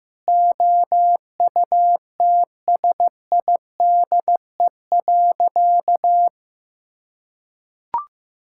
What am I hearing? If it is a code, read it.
Morse code, heard as a keyed tone, about 15 wpm, 700 Hz. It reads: OUTSIDE.
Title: Sentences from Top 1000 Words - Review Part 6 - 15wpm